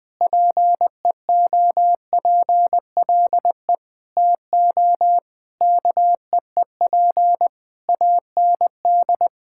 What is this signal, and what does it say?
Morse code, heard as a keyed tone, about 20 wpm, 700 Hz. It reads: PEOPLE TO KEEP AND
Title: Sentences with 8-Word Phrases - 20wpm